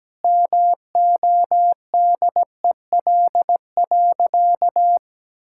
Morse code, heard as a keyed tone, about 17 wpm, 700 Hz. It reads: MODEL.
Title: Sentences from Top 1000 Words - Review Part 5 - 17wpm